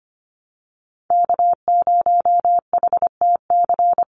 Morse code, heard as a keyed tone, about 25 wpm, 700 Hz. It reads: K0HTC
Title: USA Call Signs 25wpm